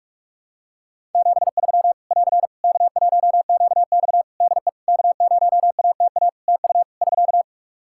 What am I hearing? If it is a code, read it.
Morse code, heard as a keyed tone, about 46 wpm, 700 Hz. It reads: 73 <AR> K1QX DE K0ATA TU <SK>
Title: Practice QSO - Exchange 4 - Ending - 45wpm